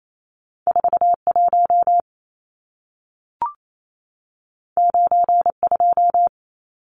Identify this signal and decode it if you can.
Morse code, heard as a keyed tone, about 28 wpm, 700 Hz. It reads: 41 92
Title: Numbers - 2-Digit 28wpm